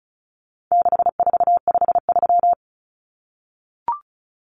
Morse code, heard as a keyed tone, about 34 wpm, 700 Hz. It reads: 6453
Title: Numbers - 4-Digit 35wpm